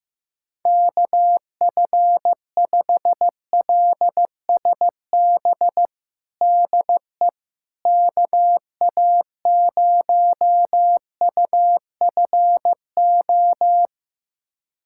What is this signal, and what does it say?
Morse code, heard as a keyed tone, about 15 wpm, 700 Hz. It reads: KF5LSB DE KA0UFO
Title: USA Callsign DE Callsign 15wpm